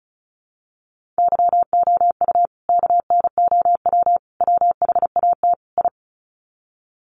Morse code, heard as a keyed tone, about 34 wpm, 700 Hz. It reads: YOU KNOW WHAT I
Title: Sets of 4 Words 35wpm